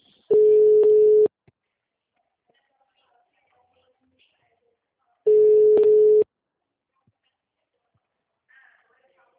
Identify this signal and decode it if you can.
Ringback tone (Europe, 425 Hz), with speech.
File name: out- -2010- - - .53640